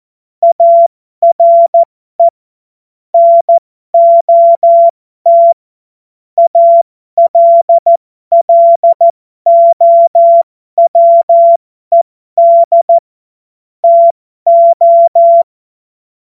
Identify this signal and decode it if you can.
Morse code, heard as a keyed tone, about 14 wpm, spaced as for 12 wpm, 680 Hz. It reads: ARE NOT ALLOWED TO
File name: Morsekurs / Witze